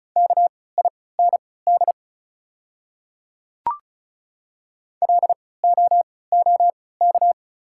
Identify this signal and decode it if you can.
Morse code, heard as a keyed tone, about 36 wpm, spaced as for 20 wpm, 690 Hz. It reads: KIND LOOK
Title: 3x Character Spacing - Top 200 Words - 35wpm